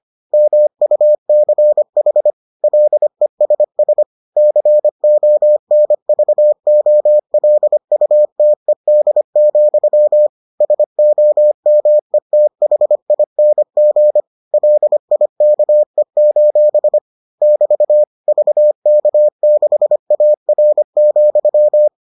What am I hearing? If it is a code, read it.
Morse code, heard as a keyed tone, about 25 wpm, 600 Hz. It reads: MUCH LESS CONVOLUTED, SOMETHING LIKE: <BT> VK6AR,